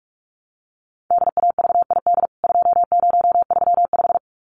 Morse code, heard as a keyed tone, about 44 wpm, 700 Hz. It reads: DAVID 2035
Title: CWT Exchange 45wpm